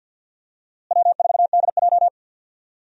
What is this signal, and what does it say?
Morse code, heard as a keyed tone, about 51 wpm, 700 Hz. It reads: W4DJ
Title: USA Call Signs - 4 Characters - Repeated 3x - 50wpm